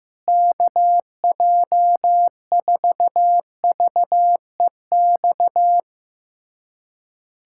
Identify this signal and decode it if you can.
Morse code, heard as a keyed tone, about 15 wpm, 700 Hz. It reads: KJ4VEX
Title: USA Callsign DE Callsign - 6 Characters - 15wpm